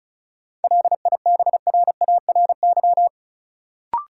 Morse code, heard as a keyed tone, about 36 wpm, 700 Hz. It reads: LIBRARY
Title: Sentences from Top 600 Words - New Word Drill - 35wpm